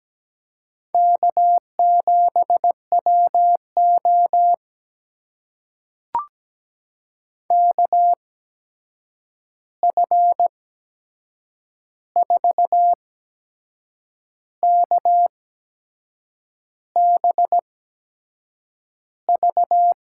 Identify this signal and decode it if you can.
Morse code, heard as a keyed tone, about 17 wpm, spaced as for 6 wpm, 700 Hz. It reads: K7WO KF4KBV
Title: USA Call Signs - 8x Character Spacing - 17wpm